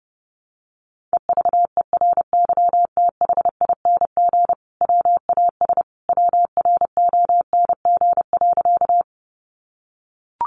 Morse code, heard as a keyed tone, about 30 wpm, 700 Hz. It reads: EVERYTHING WAS WRONG.
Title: Sentences from Top 900 Words 30wpm